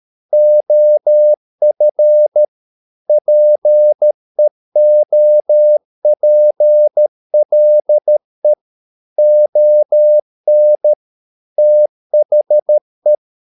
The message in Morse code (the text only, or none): OF PEOPLE ON THE